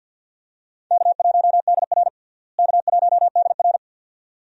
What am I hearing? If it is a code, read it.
Morse code, heard as a keyed tone, about 47 wpm, 690 Hz. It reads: K1DR K1DR
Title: USA Call Signs - 4 Characters - Repeated 2x - 50wpm